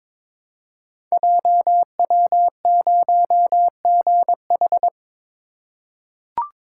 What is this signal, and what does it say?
Morse code, heard as a keyed tone, about 22 wpm, 700 Hz. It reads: JW0GH
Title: Code Groups - 5 Letter:Numbers - 22wpm